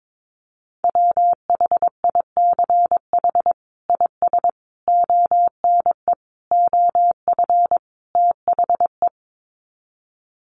Morse code, heard as a keyed tone, about 22 wpm, 700 Hz. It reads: WHICH IS ONE OF THE
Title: Sets of 5 Words Encore 22wpm